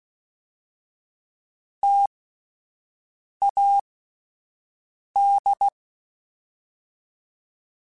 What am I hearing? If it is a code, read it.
Morse code, heard as a keyed tone, about 16 wpm, 800 Hz. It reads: T A D
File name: Difficult Combos / Difficult Combos-mp3s / T AOEI @10